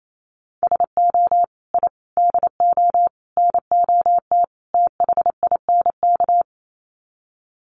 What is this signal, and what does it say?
Morse code, heard as a keyed tone, about 28 wpm, 700 Hz. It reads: SO I DO NOT THINK
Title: Sets of 5 Words 28wpm